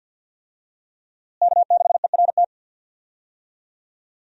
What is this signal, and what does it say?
Morse code, heard as a keyed tone, about 52 wpm, 700 Hz. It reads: K6ERT
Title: USA Call Signs - 5 Characters - Repeated 3x - 50wpm